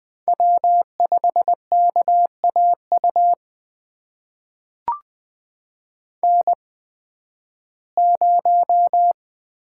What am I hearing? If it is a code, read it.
Morse code, heard as a keyed tone, about 20 wpm, spaced as for 13 wpm, 700 Hz. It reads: W5KAU N0
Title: USA Call Signs - 5 Characters - 8x Character Spacing - 20wpm